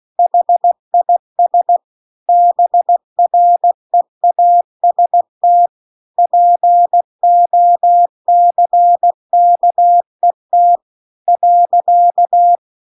Morse code, heard as a keyed tone, about 16 wpm, 700 Hz. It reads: HIS BREAST POCKET .